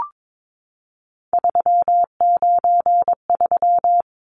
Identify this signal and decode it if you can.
Morse code, heard as a keyed tone, about 22 wpm, 700 Hz. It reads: 393
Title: Numbers - 3-Digit 22wpm